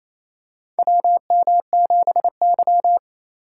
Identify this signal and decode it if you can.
Morse code, heard as a keyed tone, about 28 wpm, 700 Hz. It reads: WM7Y
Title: USA Call Signs - 4 Characters - 2x Character Spacing - 28wpm